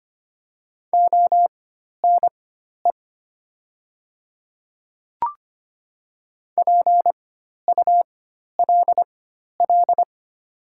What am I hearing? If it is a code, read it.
Morse code, heard as a keyed tone, about 25 wpm, spaced as for 12 wpm, 700 Hz. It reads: ONE PULL
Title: 4x Character Spacing - Top 500 Words - 25wpm